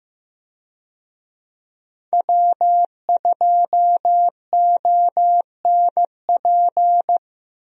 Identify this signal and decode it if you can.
Morse code, heard as a keyed tone, about 15 wpm, 700 Hz. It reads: W2ONP
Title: USA Call Signs - 5 Characters - Repeated 3x - 15wpm